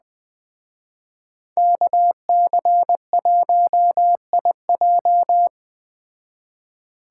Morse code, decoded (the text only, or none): KC1IJ